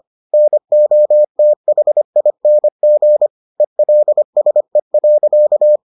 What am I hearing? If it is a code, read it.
Morse code, heard as a keyed tone, about 25 wpm, 600 Hz. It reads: NOTHING ELSE.